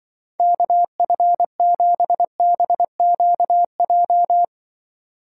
Morse code, heard as a keyed tone, about 25 wpm, 700 Hz. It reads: KF7BQJ
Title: Speed Racing - USA 2x3 Call Signs 20wpm